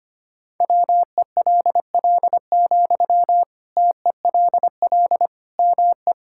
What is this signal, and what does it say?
Morse code, heard as a keyed tone, about 25 wpm, 700 Hz. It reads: WELL, TELL ME
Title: Sentences from Top 1000 Words - Review Part 6 - 25wpm